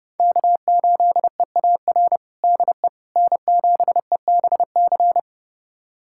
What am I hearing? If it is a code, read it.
Morse code, heard as a keyed tone, about 32 wpm, 700 Hz. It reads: K8EAR DE N7EBC
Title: USA Callsign DE Callsign - 5 Characters - 30wpm